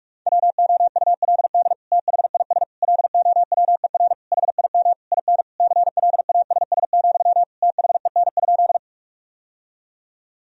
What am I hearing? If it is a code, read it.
Morse code, heard as a keyed tone, about 42 wpm, 700 Hz. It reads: WOULD THIS LOWER HIM IN CLASS, THEN?
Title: Sentences from Top 1000 Words - Review Part 4 - 45wpm